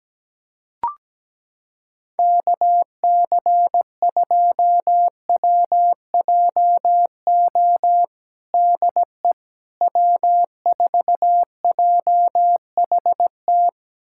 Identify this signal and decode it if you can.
Morse code, heard as a keyed tone, about 17 wpm, 700 Hz. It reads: KC2WJO DE W4JHT
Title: USA Callsign DE Callsign 17wpm